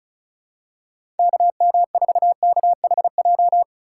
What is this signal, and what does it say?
Morse code, heard as a keyed tone, about 34 wpm, 700 Hz. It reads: KM4KHJ
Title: USA Call Signs - 4x Character Spacing - 35wpm